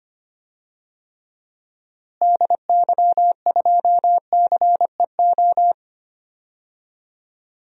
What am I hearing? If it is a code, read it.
Morse code, heard as a keyed tone, about 25 wpm, 700 Hz. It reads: DY2CEO
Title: DXCC-100 Call Signs - 6 Characters - Repeated 3x - 25wpm